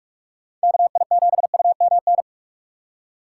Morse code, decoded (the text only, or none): KI7UMN